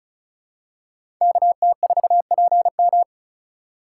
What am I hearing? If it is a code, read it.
Morse code, heard as a keyed tone, about 36 wpm, 690 Hz. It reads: KT4PM
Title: USA Call Signs - 3x Character Spacing - 35wpm